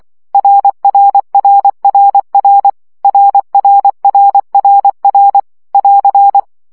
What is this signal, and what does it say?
Morse code, heard as a keyed tone, about 24 wpm, 800 Hz. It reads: RRRRR RRRRR <AR>
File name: CW gyakorlas r 20 baud